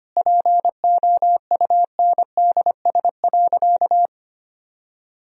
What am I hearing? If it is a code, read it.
Morse code, heard as a keyed tone, about 25 wpm, 700 Hz. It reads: POUNDS.
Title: Sentences from Top 1000 Words - Review Part 4 - 25wpm